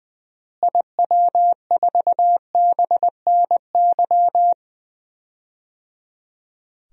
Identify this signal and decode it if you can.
Morse code, heard as a keyed tone, about 20 wpm, 700 Hz. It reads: IW4BNY
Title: DXCC-100 Call Signs - 6 Characters - Repeated 3x - 20wpm